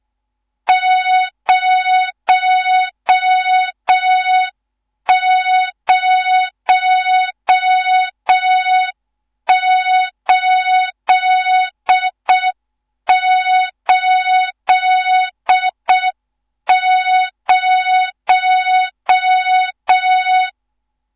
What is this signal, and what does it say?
Morse code, heard as a keyed tone, about 6 wpm, 3700 Hz, with noise. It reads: 00880
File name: Morze-HA2MN / hang / 14 280 HA2MN